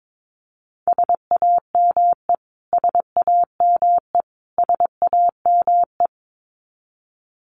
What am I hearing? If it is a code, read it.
Morse code, heard as a keyed tone, about 22 wpm, 700 Hz. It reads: SAME SAME SAME
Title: Top 300 Words Repeated 22wpm